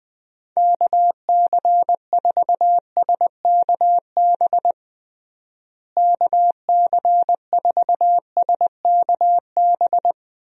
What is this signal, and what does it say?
Morse code, heard as a keyed tone, about 20 wpm, 700 Hz. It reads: KC4SKB KC4SKB
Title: USA Call Signs - 6 Characters - Repeated 2x - 20wpm